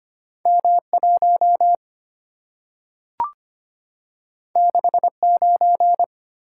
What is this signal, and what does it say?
Morse code, heard as a keyed tone, about 25 wpm, 700 Hz. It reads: M 1 6 9
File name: Two Letter-Number Combinations - 25wpm